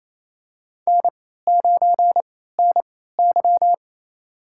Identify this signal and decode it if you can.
Morse code, heard as a keyed tone, about 28 wpm, spaced as for 16 wpm, 700 Hz. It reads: N9NY
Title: USA Call Signs - 3x Character Spacing - 28wpm